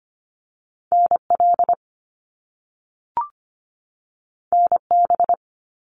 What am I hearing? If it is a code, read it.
Morse code, heard as a keyed tone, about 25 wpm, 700 Hz. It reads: NL NB
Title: Canadian Provinces and Territories 25wpm